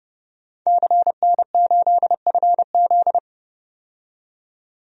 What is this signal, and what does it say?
Morse code, heard as a keyed tone, about 29 wpm, 700 Hz. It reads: CN8FZ
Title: DXCC-100 Call Signs - 5 Characters - 30wpm